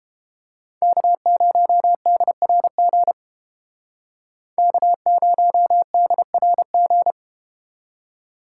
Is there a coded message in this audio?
Yes: Morse code from 0.8 to 7.1 s, heard as a keyed tone, about 32 wpm, 700 Hz. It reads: K0DRG K0DRG